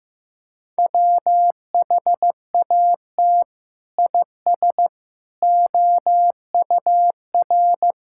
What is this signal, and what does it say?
Morse code, heard as a keyed tone, about 15 wpm, 700 Hz. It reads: WHAT IS OUR
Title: Sentences from Top 1000 Words - Review Part 5 - 15wpm